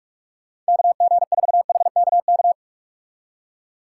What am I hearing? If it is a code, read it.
Morse code, heard as a keyed tone, about 59 wpm, spaced as for 51 wpm, 700 Hz. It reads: KG4HKK
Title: Speed Racing - USA 2x3 Call Signs 45wpm